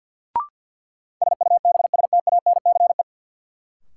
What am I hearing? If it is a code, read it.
Morse code, heard as a keyed tone, about 51 wpm, 700 Hz. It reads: SUBSTANCE
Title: Sentences from Top 1500 Words - New Word Drill Alt - 50wpm